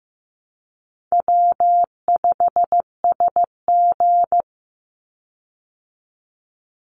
Morse code, heard as a keyed tone, about 15 wpm, 700 Hz. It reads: W5SG
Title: Easy Call Signs Repeated 2x - 15wpm